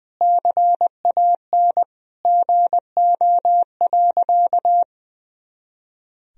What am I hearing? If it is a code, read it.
Morse code, heard as a keyed tone, about 20 wpm, 700 Hz. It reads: CAN GO.
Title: Sentences from Top 100 Words and Abbreviations 20wpm